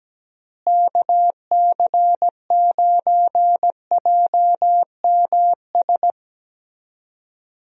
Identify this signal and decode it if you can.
Morse code, heard as a keyed tone, about 17 wpm, 700 Hz. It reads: KC9JMS
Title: USA Call Signs - 6 Characters - Repeated 2x - 17wpm